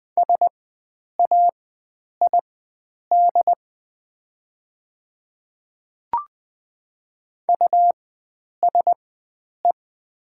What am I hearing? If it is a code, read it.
Morse code, heard as a keyed tone, about 20 wpm, spaced as for 9 wpm, 700 Hz. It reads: SAID USE
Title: 4x Character Spacing - Top 100 Words - 20wpm